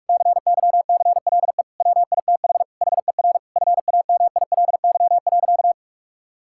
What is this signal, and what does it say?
Morse code, heard as a keyed tone, about 46 wpm, 700 Hz. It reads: CYCLE WITH HER FAMILY.
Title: Sentences from Top 1000 Words - Review Part 6 - 45wpm